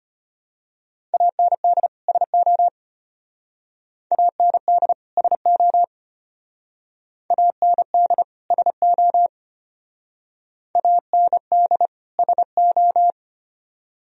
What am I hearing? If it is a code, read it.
Morse code, heard as a keyed tone, about 31 wpm, 700 Hz. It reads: AND SO AND SO AND SO AND SO
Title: Speed-Racing - Sets of 2 Words 25wpm